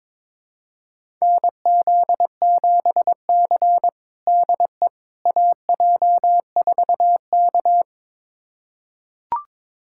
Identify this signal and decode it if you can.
Morse code, heard as a keyed tone, about 22 wpm, 700 Hz. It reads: NZ7C DE AJ4K
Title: USA Callsign DE Callsign - 4 Characters - 22wpm